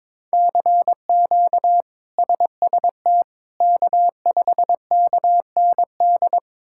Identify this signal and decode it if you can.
Morse code, heard as a keyed tone, about 22 wpm, 700 Hz. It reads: CQ SST K5KND
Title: K1USN SST QSO Practice - 22wpm